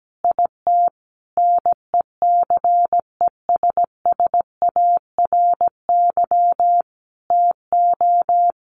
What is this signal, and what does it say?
Morse code, heard as a keyed tone, about 17 wpm, 700 Hz. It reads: IT NECESSARY TO
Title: Sentences from Top 1300 Words 17wpm